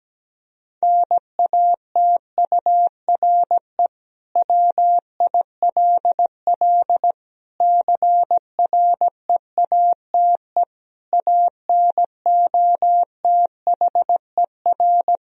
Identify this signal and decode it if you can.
Morse code, heard as a keyed tone, about 17 wpm, 700 Hz. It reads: NATURE WILL CREATE ANOTHER
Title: Sentences from Top 1000 Words - Review Part 3 - 17wpm